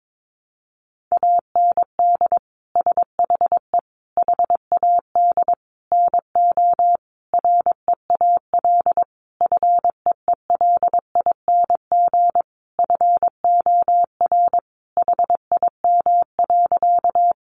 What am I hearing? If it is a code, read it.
Morse code, heard as a keyed tone, about 22 wpm, 700 Hz. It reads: AND SHE HAD NO REAL FEELING FOR HIM.
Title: Sentences from Top 1200 Words 22wpm